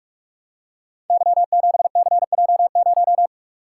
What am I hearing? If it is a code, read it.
Morse code, heard as a keyed tone, about 44 wpm, 700 Hz. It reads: Y7CJ0
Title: Code Groups - 5 Letter:Numbers - 45wpm